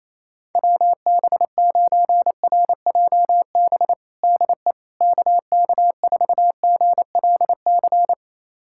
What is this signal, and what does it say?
Morse code, heard as a keyed tone, about 28 wpm, 700 Hz. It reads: WB9RJB DE KK4GLC
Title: USA Callsign DE Callsign 28wpm